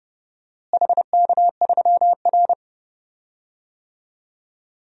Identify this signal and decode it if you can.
Morse code, heard as a keyed tone, about 30 wpm, 700 Hz. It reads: HK3R
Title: DXCC-100 Call Signs - 4 Characters - Repeated 3x - 30wpm